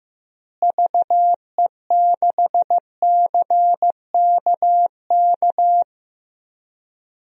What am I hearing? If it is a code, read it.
Morse code, heard as a keyed tone, about 15 wpm, 700 Hz. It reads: VE6CKK
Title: DXCC-100 Call Signs - 6 Characters - Repeated 3x - 15wpm